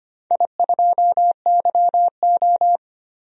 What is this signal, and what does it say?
Morse code, heard as a keyed tone, about 25 wpm, 690 Hz. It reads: I2YO
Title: DXCC-100 Call Signs - 4 Characters - 25wpm